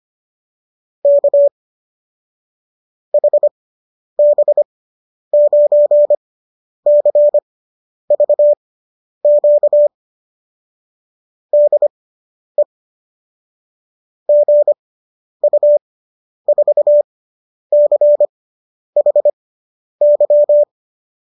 Morse code, heard as a keyed tone, about 25 wpm, spaced as for 10 wpm, 590 Hz. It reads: K HB9CVQ DE GU4CHY